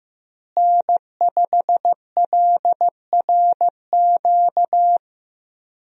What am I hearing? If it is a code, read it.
Morse code, heard as a keyed tone, about 15 wpm, 700 Hz. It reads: N5LRQ
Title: USA Call Signs - 5 Characters - Repeated 2x - 15wpm